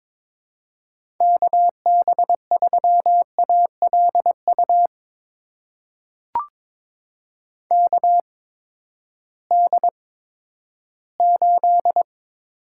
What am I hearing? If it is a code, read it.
Morse code, heard as a keyed tone, about 22 wpm, spaced as for 13 wpm, 700 Hz. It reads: KB3ALU KD8
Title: USA Call Signs - 6 Characters - 8x Character Spacing - 22wpm